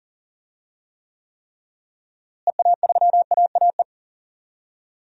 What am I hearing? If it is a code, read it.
Morse code, heard as a keyed tone, about 40 wpm, 700 Hz. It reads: EA3AAE
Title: DXCC-100 Call Signs - 6 Characters - Repeated 3x - 40wpm